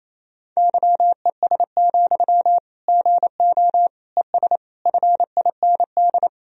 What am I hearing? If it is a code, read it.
Morse code, heard as a keyed tone, about 28 wpm, 700 Hz. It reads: YES, GO ES FIND
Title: Sentences from Top 500 Words and Abbreviations 28wpm